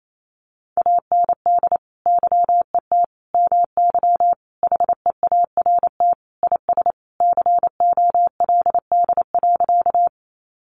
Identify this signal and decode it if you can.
Morse code, heard as a keyed tone, about 28 wpm, 700 Hz. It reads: AND YET MY HEART IS COLD.
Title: Sentences from Top 1200 Words 28wpm